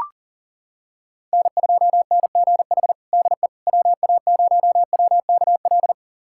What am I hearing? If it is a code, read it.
Morse code, heard as a keyed tone, about 40 wpm, 700 Hz. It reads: N2NGH DE WA0WKL
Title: USA Callsign DE Callsign 40wpm